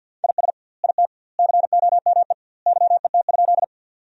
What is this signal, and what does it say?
Morse code, heard as a keyed tone, about 51 wpm, 700 Hz. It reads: IS IT COME YET?
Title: 2x Word-Spacing - Sentences from Top 500 Words - 50wpm